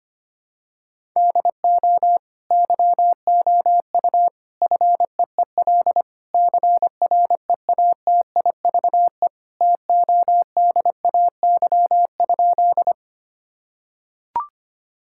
Morse code, heard as a keyed tone, about 25 wpm, 700 Hz. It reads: DO YOU FEEL CREATIVE TODAY?
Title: Sentences from Top 1000 Words - Review Part 5 - 25wpm